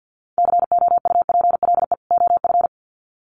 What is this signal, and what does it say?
Morse code, heard as a keyed tone, about 50 wpm, 700 Hz. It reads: COUPLE OF
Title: Sets of 2 Words 50wpm